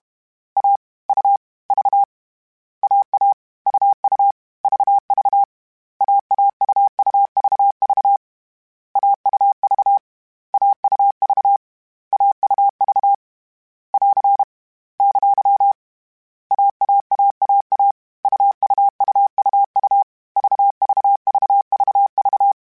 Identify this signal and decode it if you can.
Morse code, heard as a keyed tone, about 32 wpm, 800 Hz. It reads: A U V AA UU VV AAUUVV AUV AUV AUV <AR> ! AAAAA UUUUU VVVVV